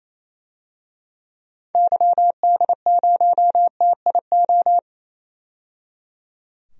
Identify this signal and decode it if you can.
Morse code, heard as a keyed tone, about 28 wpm, 700 Hz. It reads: YD0TIO
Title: DXCC-100 Call Signs - 6 Characters - 28wpm